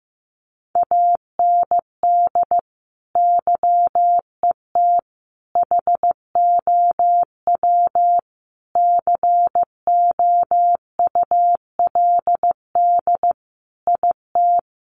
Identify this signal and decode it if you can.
Morse code, heard as a keyed tone, about 15 wpm, 700 Hz. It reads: AND YET HOW COULD IT
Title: Sentences from Top 1300 Words 15wpm